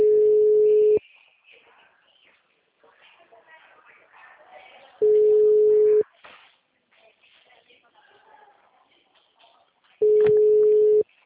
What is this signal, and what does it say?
Ringback tone (Europe, 425 Hz), with speech.